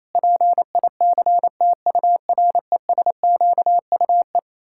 Morse code, heard as a keyed tone, about 27 wpm, 700 Hz. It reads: PICTURESQUE
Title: Pangram Challenge v2 - 28wpm